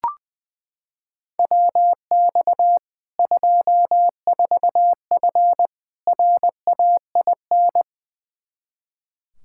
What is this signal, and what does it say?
Morse code, heard as a keyed tone, about 21 wpm, 700 Hz. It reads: WX 24F RAIN
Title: Practice QSO - Exchange 2 - Info - 20wpm